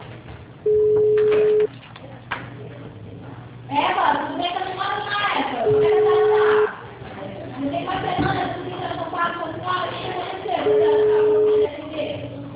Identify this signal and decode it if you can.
Ringback tone (Europe, 425 Hz), with speech.